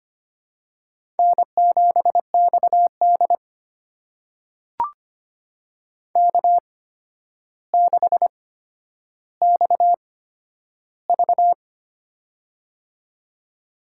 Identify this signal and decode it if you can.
Morse code, heard as a keyed tone, about 25 wpm, spaced as for 11 wpm, 700 Hz. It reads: N7XD K6XV
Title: USA Call Signs - 4 Characters - 8x Character Spacing - 25wpm